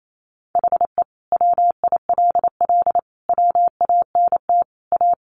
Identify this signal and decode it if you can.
Morse code, heard as a keyed tone, about 28 wpm, 700 Hz. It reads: HE WILL WANT A
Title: Sentences from Top 1300 Words 28wpm